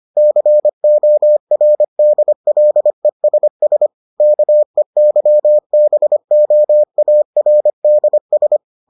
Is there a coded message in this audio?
Yes: Morse code from 0.2 to 8.6 s, heard as a keyed tone, about 25 wpm, 600 Hz. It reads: CORDLESS KEYBOARDS